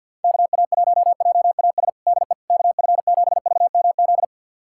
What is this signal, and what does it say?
Morse code, heard as a keyed tone, about 47 wpm, 700 Hz. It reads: KA1JAS DE KF7VMZ
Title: USA Callsign DE Callsign - 6 Characters - 50wpm